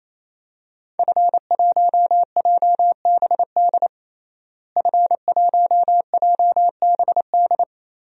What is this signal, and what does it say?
Morse code, heard as a keyed tone, about 27 wpm, 700 Hz. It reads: F1JBD F1JBD
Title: DXCC-100 Call Signs - 5 Characters - Repeated 3x - 28wpm